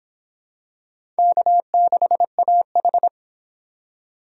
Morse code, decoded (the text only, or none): K6AH